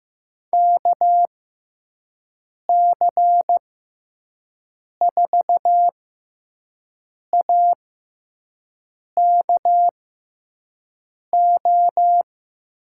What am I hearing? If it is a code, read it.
Morse code, heard as a keyed tone, about 15 wpm, spaced as for 5 wpm, 700 Hz. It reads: KC4AKO
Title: USA Call Signs - 6 Characters - 6x Character Spacing - 15wpm